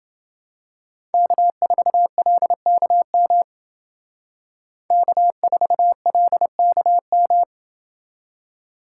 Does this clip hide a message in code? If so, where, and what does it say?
Yes: Morse code from 1.1 to 7.4 s, heard as a keyed tone, about 28 wpm, 700 Hz. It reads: K4LKM K4LKM